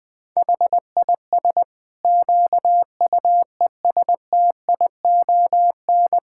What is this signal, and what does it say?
Morse code, heard as a keyed tone, about 20 wpm, 700 Hz. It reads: HIS QUESTION
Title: Sentences from Top 1000 Words - Review Part 5 - 20wpm